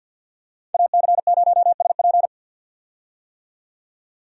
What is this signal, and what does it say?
Morse code, heard as a keyed tone, about 50 wpm, 700 Hz. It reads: AC0SP
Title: USA Call Signs - 5 Characters - 50wpm